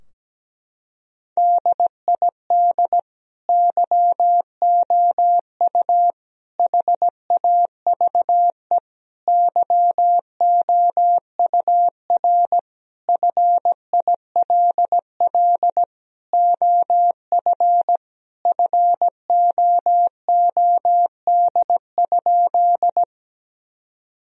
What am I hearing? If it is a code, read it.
Morse code, heard as a keyed tone, about 17 wpm, 700 Hz. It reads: DID YOU HAVE YOUR FILL OF FOOD?
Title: Sentences from Top 1000 Words - Review Part 3 - 17wpm